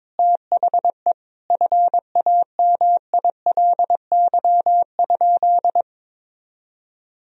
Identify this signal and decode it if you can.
Morse code, heard as a keyed tone, about 22 wpm, 700 Hz. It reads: THE FAMILY?
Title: Sentences from Top 1000 Words - Review Part 4 - 22wpm